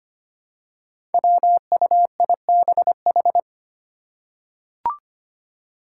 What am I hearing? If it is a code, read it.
Morse code, heard as a keyed tone, about 25 wpm, 700 Hz. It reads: WUIBH
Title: Five Challenging Letter-Number Combinations 25wpm